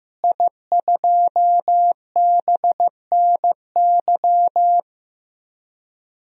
Morse code, heard as a keyed tone, about 15 wpm, 700 Hz. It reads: I2BNY
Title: DXCC-100 Call Signs - 5 Characters - Repeated 3x - 15wpm